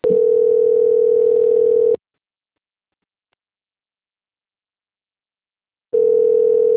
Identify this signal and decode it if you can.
Ringback tone (North America), 440 + 480 Hz.
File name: rg-1600-unknown- - - .48654